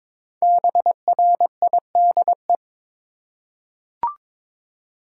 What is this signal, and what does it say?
Morse code, heard as a keyed tone, about 22 wpm, 700 Hz. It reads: BRIDE
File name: 5 Letter Words - First 3 Letters Given - 22wpm